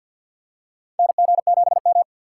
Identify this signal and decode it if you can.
Morse code, heard as a keyed tone, about 51 wpm, 700 Hz. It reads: NG7M
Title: USA Call Signs - 4 Characters - 3x Character Spacing - 50wpm